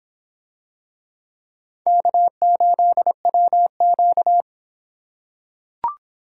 Morse code, heard as a keyed tone, about 24 wpm, 700 Hz. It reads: K8WQ